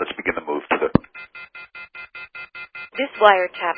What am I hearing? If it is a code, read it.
Off-hook warning tone, 1400 + 2060 + 2450 + 2600 Hz, with speech.